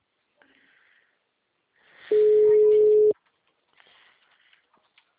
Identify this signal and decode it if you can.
Ringback tone (Europe, 425 Hz), with speech.